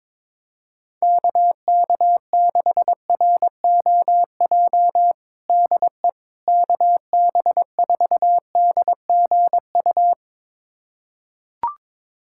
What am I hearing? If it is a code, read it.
Morse code, heard as a keyed tone, about 21 wpm, 700 Hz. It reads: KK6ROJ DE KB4DGU
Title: USA Callsign DE Callsign - 6 Characters - 22wpm